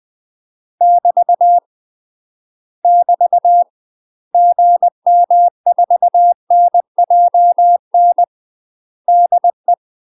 Morse code, heard as a keyed tone, about 20 wpm, 700 Hz. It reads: <BT> <BT> GM4NJN DE
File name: UK MASTER QSOs TRAINING / 20WPM / UK MASTER QSOs 0070